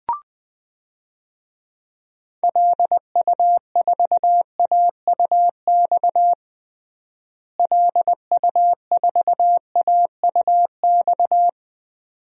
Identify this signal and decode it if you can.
Morse code, heard as a keyed tone, about 21 wpm, 700 Hz. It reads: LU4AUX LU4AUX
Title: DXCC-100 Call Signs - 6 Characters - Repeated 2x - 20wpm